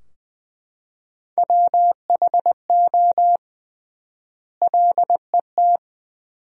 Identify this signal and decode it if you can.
Morse code, heard as a keyed tone, about 21 wpm, 700 Hz. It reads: WHO LET
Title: 3x Word-Spacing - Sentences from Top 300 Words - 20wpm